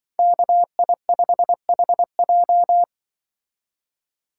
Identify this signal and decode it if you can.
Morse code, heard as a keyed tone, about 24 wpm, 700 Hz. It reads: KI5HJ
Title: Speed Racing - USA 2x2 Call Signs 20wpm